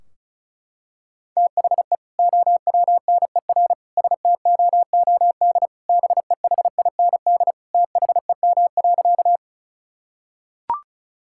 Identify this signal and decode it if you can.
Morse code, heard as a keyed tone, about 36 wpm, 700 Hz. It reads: THE OWNER STOOD BEHIND THEM.
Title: Sentences from Top 1600 Words 35wpm